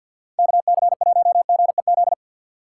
Morse code, heard as a keyed tone, about 52 wpm, 700 Hz. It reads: KC1GEZ
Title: USA Call Signs - 4x Character Spacing - 50wpm